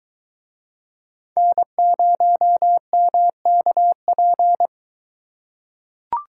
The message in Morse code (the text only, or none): N0MKP